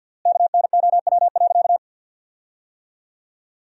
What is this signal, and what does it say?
Morse code, heard as a keyed tone, about 53 wpm, 700 Hz. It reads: KNOW.
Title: 3x Word-Spacing - Sentences from Top 300 Words - 50wpm